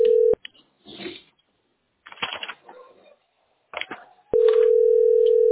Ringback tone (North America), 440 + 480 Hz.